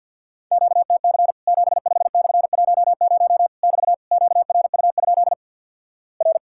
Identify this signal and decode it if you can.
Morse code, heard as a keyed tone, about 50 wpm, 700 Hz. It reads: QTC 75/10 <BT> QRU? R
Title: Worked All Europe DX Contest - QTC Exchange with Repeats - 50wpm